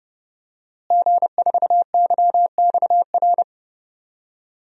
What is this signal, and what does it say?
Morse code, heard as a keyed tone, about 30 wpm, 700 Hz. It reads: G4YXR
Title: DXCC-100 Call Signs - 5 Characters - Repeated 2x - 30wpm